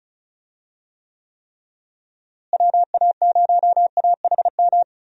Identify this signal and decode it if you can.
Morse code, heard as a keyed tone, about 36 wpm, 700 Hz. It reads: WA0AHM
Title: USA Call Signs - 6 Characters - Repeated 3x - 35wpm